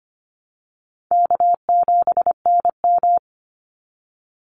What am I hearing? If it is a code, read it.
Morse code, heard as a keyed tone, about 25 wpm, 700 Hz. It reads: K7NM
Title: Easy Call Signs Repeated 3x - 25wpm